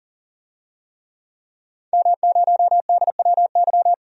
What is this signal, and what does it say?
Morse code, heard as a keyed tone, about 40 wpm, 700 Hz. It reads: M0DWY
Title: DXCC-100 Call Signs - 5 Characters - 40wpm